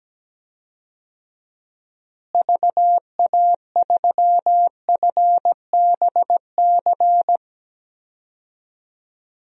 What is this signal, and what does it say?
Morse code, heard as a keyed tone, about 17 wpm, 700 Hz. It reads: VA3FBC
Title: DXCC-100 Call Signs - 6 Characters - Repeated 2x - 17wpm